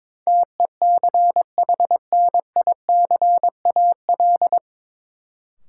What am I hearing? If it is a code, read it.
Morse code, heard as a keyed tone, about 21 wpm, 700 Hz. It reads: TECHNICAL
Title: Sentences from Top 1100 Words - New Word Drill - 22wpm